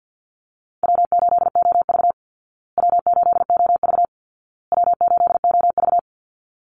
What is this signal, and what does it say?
Morse code, heard as a keyed tone, about 48 wpm, 700 Hz. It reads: W8OV W8OV W8OV
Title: Easy Call Signs Repeated 3x - 50wpm